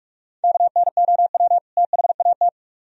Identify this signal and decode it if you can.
Morse code, heard as a keyed tone, about 61 wpm, spaced as for 53 wpm, 700 Hz. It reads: KNOW THAT
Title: Speed-Racing - Sets of 2 Words 45wpm